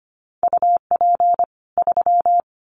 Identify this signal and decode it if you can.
Morse code, heard as a keyed tone, about 25 wpm, 700 Hz. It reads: UP 3
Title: Common QSO Phrases 25wpm